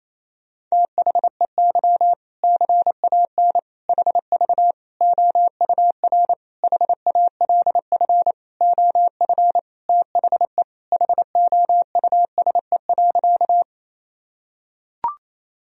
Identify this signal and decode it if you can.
Morse code, heard as a keyed tone, about 27 wpm, 700 Hz. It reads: THEY CAN HV OUR HALF OF THE HOUSE.
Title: Sentences from Top 400 Words and Abbreviations 28wpm